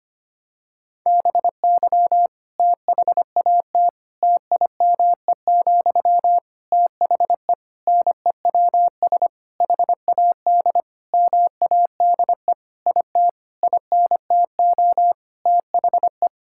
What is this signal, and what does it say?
Morse code, heard as a keyed tone, about 25 wpm, 700 Hz. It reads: BY THAT TIME, THE NEWS HAD MADE IT INTO THE
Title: Sentences from Top 1600 Words 25wpm